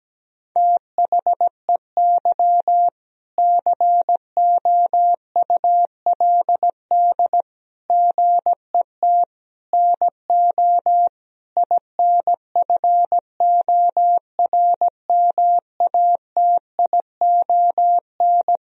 Morse code, heard as a keyed tone, about 17 wpm, 700 Hz. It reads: THEY COULD GET NO INFORMATION
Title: Sentences from Top 1000 Words - Review Part 3 - 17wpm